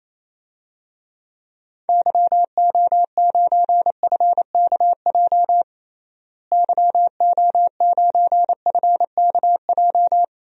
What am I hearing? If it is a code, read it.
Morse code, heard as a keyed tone, about 28 wpm, 700 Hz. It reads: YO9FKJ YO9FKJ
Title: DXCC-100 Call Signs - 6 Characters - Repeated 2x - 28wpm